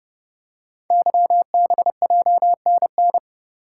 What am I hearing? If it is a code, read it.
Morse code, heard as a keyed tone, about 30 wpm, 700 Hz. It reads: YBJNN
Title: Code Groups - 5 Letters - 30wpm